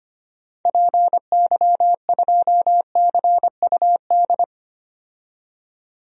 Morse code, heard as a keyed tone, about 25 wpm, 700 Hz. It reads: PY2CUD
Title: DXCC-100 Call Signs - 6 Characters - 25wpm